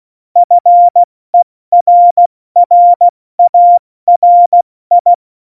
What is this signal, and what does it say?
Morse code, heard as a keyed tone, about 16 wpm, 700 Hz. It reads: FERRARI
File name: Morsekurs / Witze / Witz 04 August 2020, 70 F80